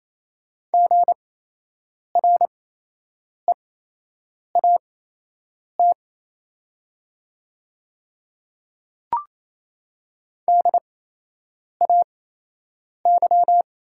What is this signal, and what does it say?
Morse code, heard as a keyed tone, about 27 wpm, spaced as for 8 wpm, 700 Hz. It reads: GREAT DAY
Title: 8x Character Spacing - Top 200 Words - 28wpm